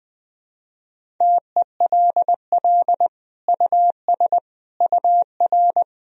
Morse code, heard as a keyed tone, about 20 wpm, 700 Hz. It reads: TELL US UR
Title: Sentences from Top 200 Words and Abbreviations 20wpm